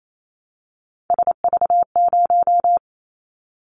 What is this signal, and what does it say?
Morse code, heard as a keyed tone, about 28 wpm, 700 Hz. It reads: SV0
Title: Callsign Prefixes for Top 50 DXCC - 28wpm